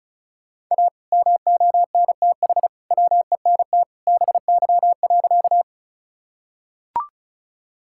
Morse code, heard as a keyed tone, about 36 wpm, 700 Hz. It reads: A MONTH WENT BY.
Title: Sentences from Top 1000 Words - Review Part 5 - 35wpm